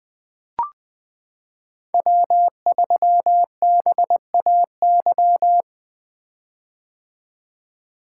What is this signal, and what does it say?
Morse code, heard as a keyed tone, about 20 wpm, 700 Hz. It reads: W3BAY
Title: 2023 POTA Activator Call Signs - 20wpm